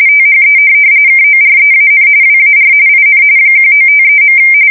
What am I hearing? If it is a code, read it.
RTTY, about 45.45 baud. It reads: THE KOREAN WAR IN 1953.
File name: fsk170 / USS-New-Jersey